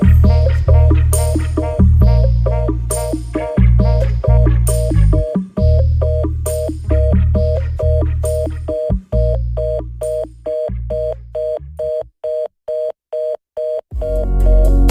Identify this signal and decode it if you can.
Reorder tone (North America), 480 + 620 Hz, with noise.